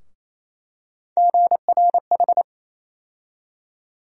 Morse code, heard as a keyed tone, about 27 wpm, 700 Hz. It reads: GRH
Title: USA Call Sign Postfixes - 3 Letters 28wpm